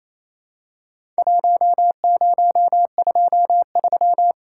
Morse code, heard as a keyed tone, about 28 wpm, 700 Hz. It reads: 1023
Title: Worked All Europe DX Contest - QTC Exchange - 28wpm